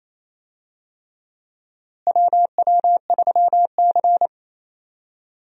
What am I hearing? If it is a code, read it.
Morse code, heard as a keyed tone, about 28 wpm, 700 Hz. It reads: WW3C
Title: USA Call Signs - 4 Characters - Repeated 3x - 28wpm